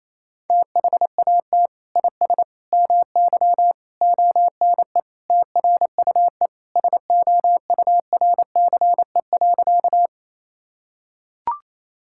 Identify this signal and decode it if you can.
Morse code, heard as a keyed tone, about 28 wpm, 700 Hz. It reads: THAT IS MY ONE TRUE SOURCE.
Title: Sentences from Top 1000 Words - Review Part 5 - 28wpm